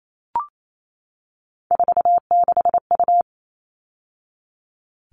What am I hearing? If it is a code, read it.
Morse code, heard as a keyed tone, about 28 wpm, 700 Hz. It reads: 46U
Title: All Cut Numbers - 3-Digit 28wpm